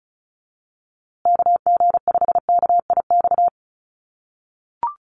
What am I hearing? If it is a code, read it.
Morse code, heard as a keyed tone, about 34 wpm, 700 Hz. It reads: KG5KIX
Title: USA Call Signs Repeated 2x 35wpm